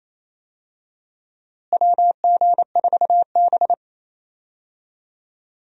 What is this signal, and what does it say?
Morse code, heard as a keyed tone, about 28 wpm, 700 Hz. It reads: WG4B
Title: USA Call Signs - 4 Characters - Repeated 2x - 28wpm